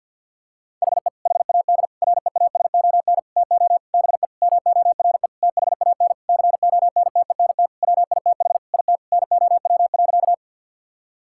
Morse code, heard as a keyed tone, about 51 wpm, 700 Hz. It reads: HE HAD REASON TO BE MORE THAN CONTENT WITH IT NOW.
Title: Sentences from Top 1000 Words - Review Part 6 - 50wpm